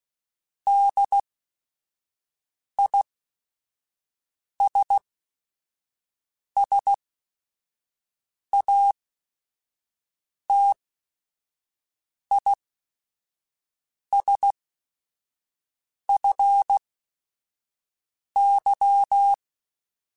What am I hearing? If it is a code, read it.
Morse code, heard as a keyed tone, about 16 wpm, spaced as for 5 wpm, 800 Hz. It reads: DISSATISFY